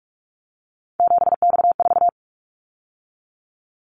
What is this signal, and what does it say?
Morse code, heard as a keyed tone, about 46 wpm, 700 Hz. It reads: 7X4
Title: Callsign Prefixes for Top 100 DXCC - 45wpm